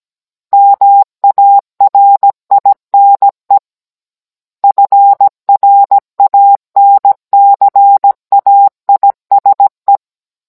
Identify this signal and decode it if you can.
Morse code, heard as a keyed tone, about 17 wpm, 800 Hz. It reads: MARINE FRANCAISE
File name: Cours-NR19-16mm